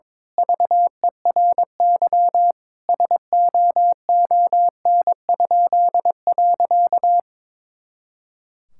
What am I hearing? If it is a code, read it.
Morse code, heard as a keyed tone, about 22 wpm, 700 Hz. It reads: VERY SOON?.
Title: Pangram Challenge v2 - 22wpm